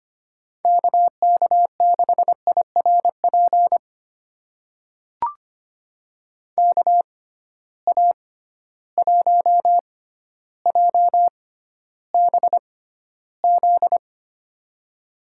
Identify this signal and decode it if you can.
Morse code, heard as a keyed tone, about 25 wpm, spaced as for 13 wpm, 700 Hz. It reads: KK6IRP KA1JBZ
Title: USA Call Signs - 6 Characters - 6x Character Spacing - 25wpm